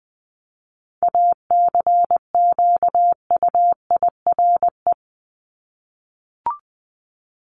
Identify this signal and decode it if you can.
Morse code, heard as a keyed tone, about 20 wpm, 700 Hz. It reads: ACQUIRE
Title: Top 1501-1750 Words 20wpm